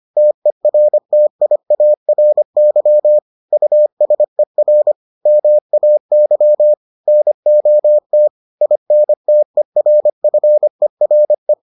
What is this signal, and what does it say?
Morse code, heard as a keyed tone, about 25 wpm, 600 Hz. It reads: TERTIARY USER MAY NOT INTERFERE